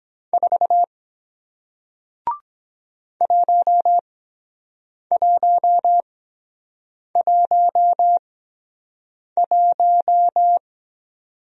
Morse code, heard as a keyed tone, about 23 wpm, spaced as for 7 wpm, 700 Hz. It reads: 4 1111